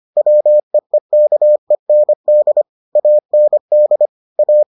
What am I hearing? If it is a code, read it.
Morse code, heard as a keyed tone, about 25 wpm, 600 Hz. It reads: WEEKEND AND A